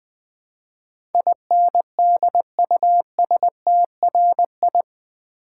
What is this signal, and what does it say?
Morse code, heard as a keyed tone, about 20 wpm, 700 Hz. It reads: INDUSTRI
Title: Build A Word - 4 Letters At A Time - 10-Letter Words - 20wpm